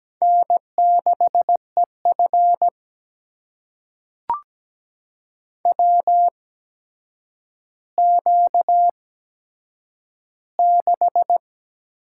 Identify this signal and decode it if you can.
Morse code, heard as a keyed tone, about 17 wpm, spaced as for 8 wpm, 700 Hz. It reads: N6EF WQ6
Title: USA Call Signs - 4 Characters - 8x Character Spacing - 17wpm